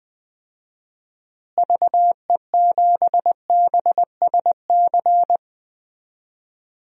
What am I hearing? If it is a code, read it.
Morse code, heard as a keyed tone, about 20 wpm, 700 Hz. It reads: VE7BSC
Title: DXCC-100 Call Signs - 6 Characters - Repeated 2x - 20wpm